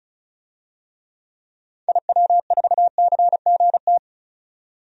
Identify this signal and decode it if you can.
Morse code, heard as a keyed tone, about 36 wpm, 700 Hz. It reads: IW4CGT
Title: DXCC-100 Call Signs - 6 Characters - 35wpm